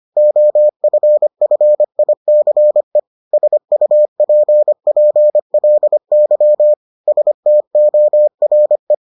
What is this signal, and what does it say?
Morse code, heard as a keyed tone, about 25 wpm, 600 Hz. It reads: OFFICE SUPPLY STORE